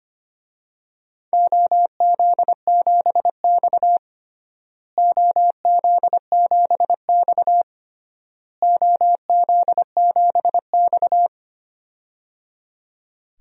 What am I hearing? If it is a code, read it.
Morse code, heard as a keyed tone, about 25 wpm, 700 Hz. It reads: OZ7X OZ7X OZ7X
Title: DXCC-100 Call Signs - 4 Characters - Repeated 3x - 25wpm